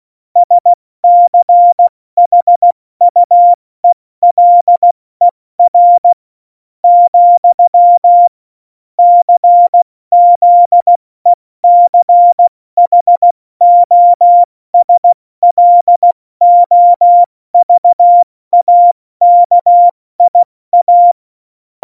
Morse code, heard as a keyed tone, about 16 wpm, 700 Hz. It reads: SCHUELER , CZECHOSLOVAKIA
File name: Morsekurs / Witze / Witz 07. Mai 2019, 70 F80